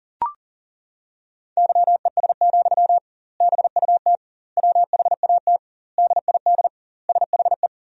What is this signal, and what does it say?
Morse code, heard as a keyed tone, about 37 wpm, 700 Hz. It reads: YES, BUT WHAT DID SHE
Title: 2x Word-Spacing - Sentences from Top 500 Words - 40wpm